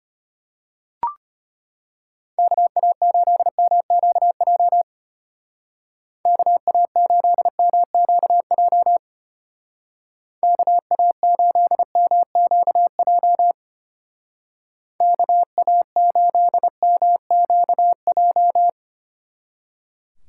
Morse code, heard as a keyed tone, about 32 wpm, 700 Hz. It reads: KA8MQJ KA8MQJ KA8MQJ KA8MQJ